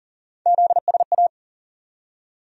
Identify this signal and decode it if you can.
Morse code, heard as a keyed tone, about 40 wpm, 700 Hz. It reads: ZSA
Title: USA Call Sign Postfixes - 3 Letters 40wpm